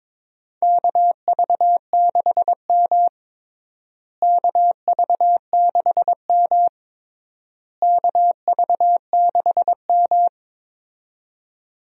Morse code, heard as a keyed tone, about 22 wpm, 700 Hz. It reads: KV6M KV6M KV6M
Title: USA Call Signs - 4 Characters - Repeated 3x - 22wpm